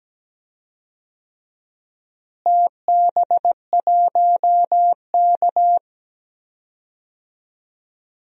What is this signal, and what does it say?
Morse code, heard as a keyed tone, about 17 wpm, 700 Hz. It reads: TB1K
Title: DXCC-100 Call Signs - 4 Characters - Repeated 2x - 17wpm